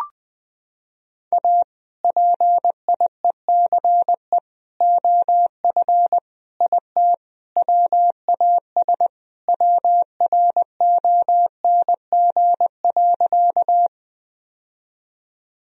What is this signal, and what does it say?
Morse code, heard as a keyed tone, about 21 wpm, 700 Hz. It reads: A PIECE OF IT WAS WRONG.
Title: Sentences from Top 1000 Words - Review Part 2 - 20wpm